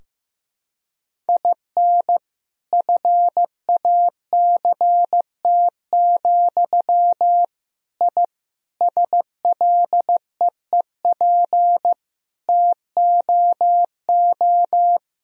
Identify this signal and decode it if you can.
Morse code, heard as a keyed tone, about 15 wpm, 700 Hz. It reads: IN FACT, I SLEEP TOO
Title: Sentences from Top 1000 Words - Review Part 3 - 15wpm